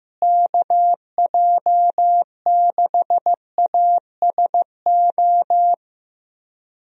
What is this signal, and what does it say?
Morse code, heard as a keyed tone, about 15 wpm, 700 Hz. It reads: KJ6ASO
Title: USA Call Signs - 6 Characters - Repeated 3x - 15wpm